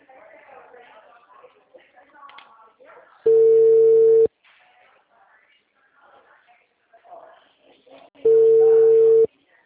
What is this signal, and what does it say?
Ringback tone (China, 450 Hz), with speech.
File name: out- -2010